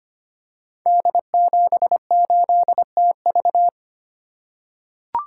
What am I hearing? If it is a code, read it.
Morse code, heard as a keyed tone, about 25 wpm, 700 Hz. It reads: D78TV
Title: Code Groups - 5 Letter:Numbers - 25wpm